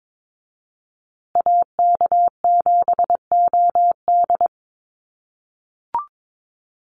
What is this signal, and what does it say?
Morse code, heard as a keyed tone, about 22 wpm, 700 Hz. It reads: AK7OD
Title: USA Call Signs Repeated 3x 22wpm